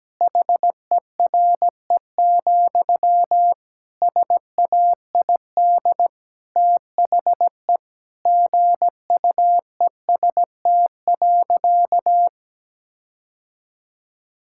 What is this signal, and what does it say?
Morse code, heard as a keyed tone, about 17 wpm, 700 Hz. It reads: HERE, SAID THE GUEST.
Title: Sentences from Top 1600 Words 17wpm